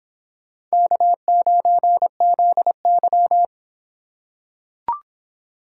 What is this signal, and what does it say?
Morse code, heard as a keyed tone, about 23 wpm, 700 Hz. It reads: K9ZY